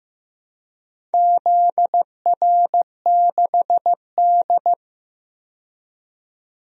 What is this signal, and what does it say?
Morse code, heard as a keyed tone, about 15 wpm, 700 Hz. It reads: ZR6D
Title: DXCC-100 Call Signs - 4 Characters - 15wpm